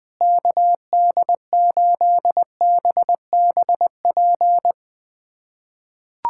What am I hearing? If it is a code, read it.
Morse code, heard as a keyed tone, about 20 wpm, 700 Hz. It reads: KD8BBP
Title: USA Callsign DE Callsign 20wpm